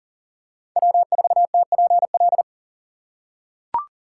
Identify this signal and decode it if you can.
Morse code, heard as a keyed tone, about 40 wpm, 700 Hz. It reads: W4TPL
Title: USA Call Signs - 5 Characters - 6x Character Spacing - 40wpm